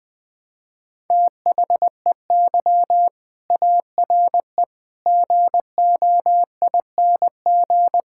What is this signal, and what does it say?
Morse code, heard as a keyed tone, about 20 wpm, 700 Hz. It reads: THEY ARE GOING
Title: Sentences with 4-Word Phrases - Spoken Beginning - 20wpm